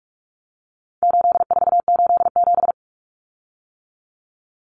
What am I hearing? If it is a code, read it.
Morse code, heard as a keyed tone, about 44 wpm, 700 Hz. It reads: 8487
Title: Common Cut Numbers - 4-Digit 45wpm